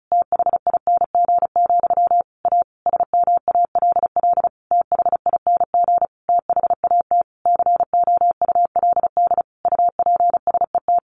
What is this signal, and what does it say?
Morse code, heard as a keyed tone, about 34 wpm, 700 Hz. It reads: THING, A SMALL THING THAT COULD UPSET
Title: A Sound Of Thunder 35wpm